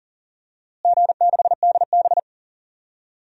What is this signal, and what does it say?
Morse code, heard as a keyed tone, about 40 wpm, 700 Hz. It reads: G6DB
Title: DXCC-100 Call Signs - 4 Characters - Repeated 2x - 40wpm